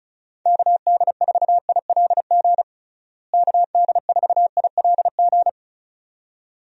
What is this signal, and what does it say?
Morse code, heard as a keyed tone, about 36 wpm, 700 Hz. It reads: KD4ILG KD4ILG
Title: USA Call Signs - 6 Characters - Repeated 2x - 35wpm